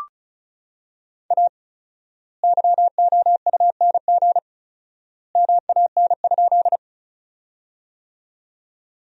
Morse code, heard as a keyed tone, about 34 wpm, 700 Hz. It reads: A YOUNG MAN?
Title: 4x Word-Spacing - Sentences from Top 400 Words - 35wpm